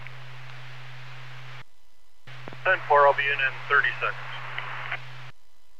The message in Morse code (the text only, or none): none